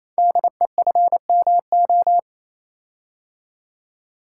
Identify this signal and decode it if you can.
Morse code, heard as a keyed tone, about 28 wpm, 700 Hz. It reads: DEFMO
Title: Code Groups - 5 Letters - 28wpm